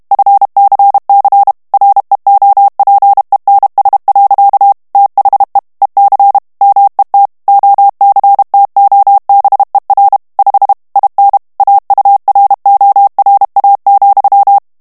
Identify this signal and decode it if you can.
Morse code, heard as a keyed tone, about 32 wpm, 800 Hz. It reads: FCC REOPENS. THE EC MET OCTOBER 5 IN AURORA,
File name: ANGLAIS texte clair 3 vitesse 30 mots minute